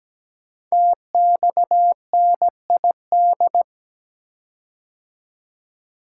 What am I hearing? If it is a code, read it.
Morse code, heard as a keyed tone, about 17 wpm, 700 Hz. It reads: TXNID
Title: Code Groups - 5 Letter:Numbers - 17wpm